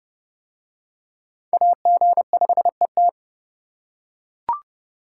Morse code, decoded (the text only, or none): AG5ET